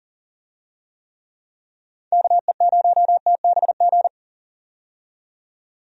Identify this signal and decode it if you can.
Morse code, heard as a keyed tone, about 40 wpm, 700 Hz. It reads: KE0TBG
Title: USA Call Signs - 6 Characters - 40wpm